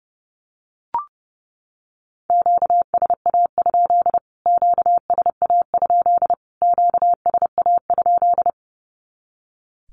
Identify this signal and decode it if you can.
Morse code, heard as a keyed tone, about 30 wpm, 700 Hz. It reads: QSA? QSA? QSA?
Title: Q-Codes Repeated 3x 30wpm